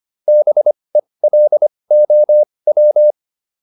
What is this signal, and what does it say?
Morse code, heard as a keyed tone, about 25 wpm, spaced as for 20 wpm, 600 Hz. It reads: BELOW